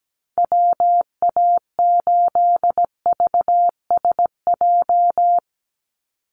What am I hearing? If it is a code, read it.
Morse code, heard as a keyed tone, about 17 wpm, 700 Hz. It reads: WA8VSJ
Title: Call Signs - 17wpm